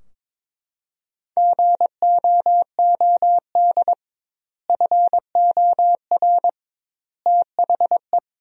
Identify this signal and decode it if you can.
Morse code, heard as a keyed tone, about 22 wpm, 700 Hz. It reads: GOOD FOR THE
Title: 2x Word-Spacing - Sentences from Top 300 Words - 22wpm